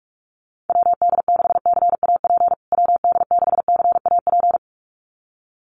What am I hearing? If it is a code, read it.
Morse code, heard as a keyed tone, about 44 wpm, 690 Hz. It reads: WD6CAP WD6CAP
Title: USA Call Signs Repeated 2x 45wpm